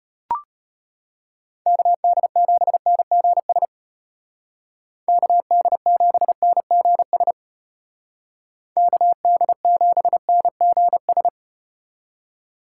Morse code, heard as a keyed tone, about 34 wpm, 700 Hz. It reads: KD7NGS KD7NGS KD7NGS